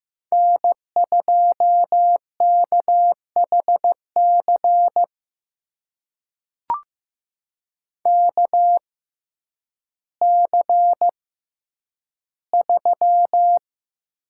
Morse code, heard as a keyed tone, about 15 wpm, spaced as for 9 wpm, 700 Hz. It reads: N2KHC KC3
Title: USA Call Signs - 6x Character Spacing - 15wpm